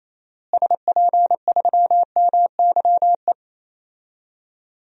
Morse code, heard as a keyed tone, about 28 wpm, 700 Hz. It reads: SP3MYE
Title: DXCC-100 Call Signs - 6 Characters - Repeated 3x - 28wpm